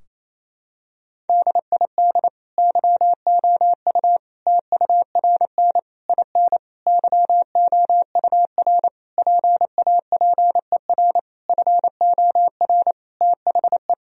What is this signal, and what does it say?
Morse code, heard as a keyed tone, about 27 wpm, 700 Hz. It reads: DID YOU TURN IN YOUR PAPER FOR THE
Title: Sentences from Top 1000 Words - Review Part 2 - 28wpm